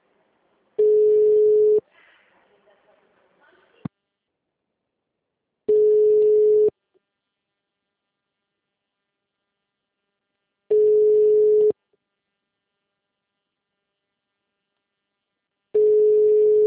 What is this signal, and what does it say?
Ringback tone (Europe, 425 Hz), with speech.
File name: out- -2012- - - .16